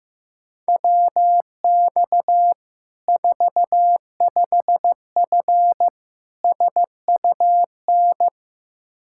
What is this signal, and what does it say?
Morse code, heard as a keyed tone, about 15 wpm, 700 Hz. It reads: WX 45F SUN
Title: Practice QSO - Exchange 2 - Info - 15wpm